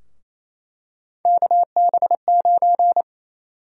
Morse code, heard as a keyed tone, about 28 wpm, 700 Hz. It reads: KB9
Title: USA Call Sign Prefixes - Repeated 2x - 28wpm